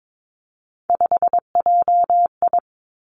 Morse code, heard as a keyed tone, about 22 wpm, 700 Hz. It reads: 5JI
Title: Three Challenging Letter-Number Combinations 22wpm